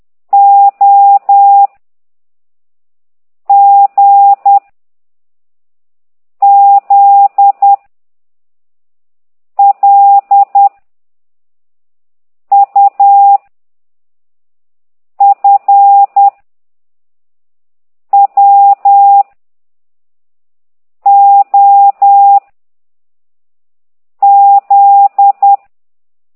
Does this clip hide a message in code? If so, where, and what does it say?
Yes: Morse code from 0.3 to 25.6 s, heard as a keyed tone, about 10 wpm, 800 Hz. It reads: O G Z L U F W O Z